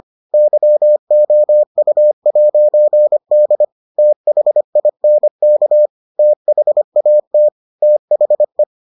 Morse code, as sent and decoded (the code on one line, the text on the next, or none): -.-- --- ..- .----. -.. / - .... .. -. -.- / - .... .- - / - .... .
YOU'D THINK THAT THE